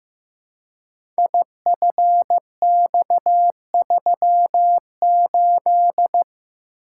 Morse code, heard as a keyed tone, about 15 wpm, 700 Hz. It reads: IFX38
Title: Code Groups - 5 Letter:Numbers - 15wpm